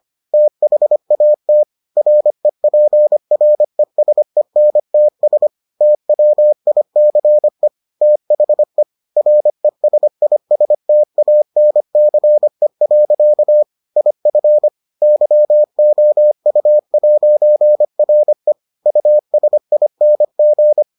Morse code, heard as a keyed tone, about 25 wpm, 600 Hz. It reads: THAT REPRESENTS TWICE THE RESISTANCE. IF YOU'RE USING